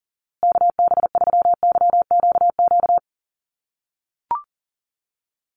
Morse code, heard as a keyed tone, about 37 wpm, 700 Hz. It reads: KB3YQQ
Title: USA Call Signs Repeated 3x 40wpm